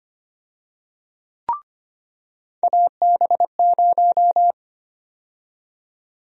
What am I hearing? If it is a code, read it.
Morse code, heard as a keyed tone, about 25 wpm, 700 Hz. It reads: AB0
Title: USA Call Sign Prefixes - 25wpm